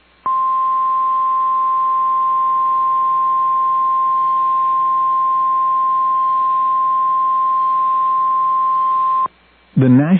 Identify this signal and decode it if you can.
Emergency Alert System (SAME) tones (the 1050 Hz attention tone), with speech. It is heard from 0.2 to 9.3 s.